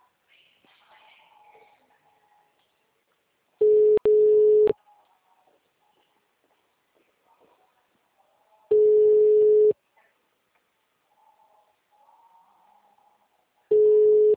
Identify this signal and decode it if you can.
Ringback tone (Europe, 425 Hz).
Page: Sao Carlos - SP